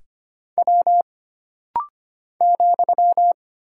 Morse code, heard as a keyed tone, about 25 wpm, 700 Hz. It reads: W ,
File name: Single Character (Letter:Number:Punctuation:Prosign) - Rapid-Fire - 25wpm